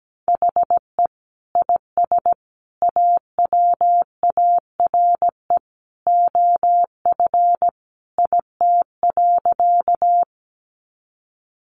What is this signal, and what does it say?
Morse code, heard as a keyed tone, about 17 wpm, 700 Hz. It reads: HE IS AWARE OF IT.
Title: Sentences from Top 1200 Words 17wpm